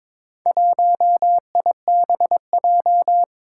Morse code, heard as a keyed tone, about 22 wpm, 700 Hz. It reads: 1IBJ
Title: Four Challenging Letter-Number Combinations 22wpm